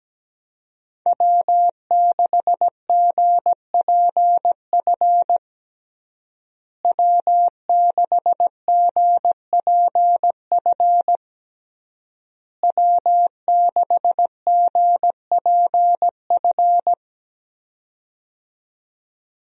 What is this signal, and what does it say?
Morse code, heard as a keyed tone, about 17 wpm, 700 Hz. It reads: W6GPF W6GPF W6GPF
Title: USA Call Signs - 5 Characters - Repeated 3x - 17wpm